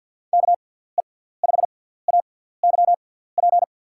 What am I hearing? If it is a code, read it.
Morse code, heard as a keyed tone, about 52 wpm, spaced as for 17 wpm, 700 Hz. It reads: KE5AYP
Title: USA Call Signs - 6x Character Spacing - 50wpm